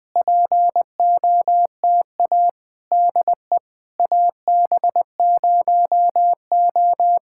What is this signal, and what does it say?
Morse code, heard as a keyed tone, about 20 wpm, 700 Hz. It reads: POTA DE AB0O
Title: POTA QSO Practice - 20wpm